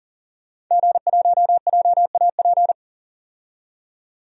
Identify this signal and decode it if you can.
Morse code, heard as a keyed tone, about 40 wpm, 700 Hz. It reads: G1JAP
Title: DXCC-100 Call Signs - 5 Characters - 40wpm